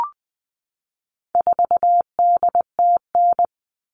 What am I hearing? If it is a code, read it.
Morse code, heard as a keyed tone, about 20 wpm, 700 Hz. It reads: 4DTN
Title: All Cut Numbers - 4-Digit 20wpm